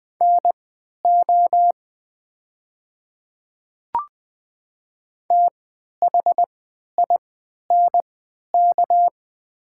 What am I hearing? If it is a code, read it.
Morse code, heard as a keyed tone, about 20 wpm, spaced as for 11 wpm, 700 Hz. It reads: NO THINK
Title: 3x Character Spacing - Top 200 Words - 20wpm